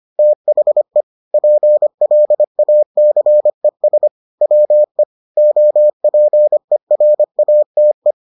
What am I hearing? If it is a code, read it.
Morse code, heard as a keyed tone, about 25 wpm, 600 Hz. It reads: THE PLACES WE OPERATE